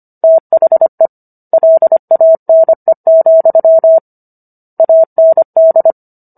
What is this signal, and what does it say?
Morse code, heard as a keyed tone, about 25 wpm, 650 Hz. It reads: THE LANE, AND
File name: Book One / 16-Chapter 16 - 25 WPM